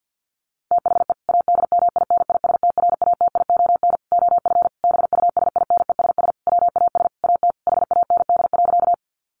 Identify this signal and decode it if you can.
Morse code, heard as a keyed tone, about 50 wpm, 700 Hz. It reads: THE ADMINISTRATION OF BUSINESS WAS AT HAND.
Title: Sentences from Top 1300 Words 50wpm